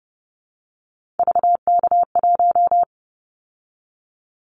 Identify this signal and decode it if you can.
Morse code, heard as a keyed tone, about 30 wpm, 700 Hz. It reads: VK1
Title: Callsign Prefixes for Top 100 DXCC - 30wpm